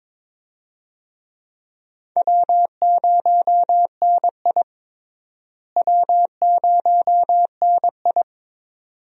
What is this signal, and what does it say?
Morse code, heard as a keyed tone, about 22 wpm, 700 Hz. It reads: W0NI W0NI
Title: USA Call Signs - 4 Characters - Repeated 2x - 22wpm